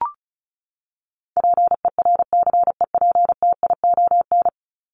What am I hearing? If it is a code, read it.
Morse code, heard as a keyed tone, about 34 wpm, 700 Hz. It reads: PERCEPTION
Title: Top 1751-2000 Words 35wpm